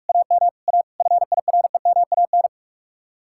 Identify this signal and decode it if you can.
Morse code, heard as a keyed tone, about 56 wpm, 700 Hz. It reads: AM A FIREMAN